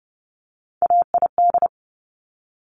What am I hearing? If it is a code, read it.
Morse code, heard as a keyed tone, about 30 wpm, 700 Hz. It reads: AID
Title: Top Three Letter Combinations in Words 30wpm